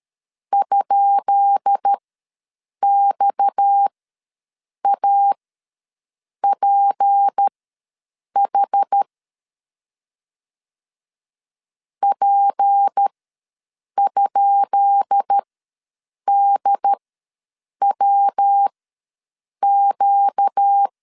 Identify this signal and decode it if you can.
Morse code, heard as a keyed tone, about 13 wpm, spaced as for 7 wpm, 800 Hz. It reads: ?XAPH P?DWQ
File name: LEZIONI / 10 SIGNS / 10B INT ws 12wpm 3527